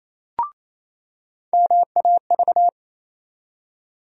Morse code, heard as a keyed tone, about 28 wpm, 700 Hz. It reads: MAV
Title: USA Call Sign Postfixes - 3 Letters 28wpm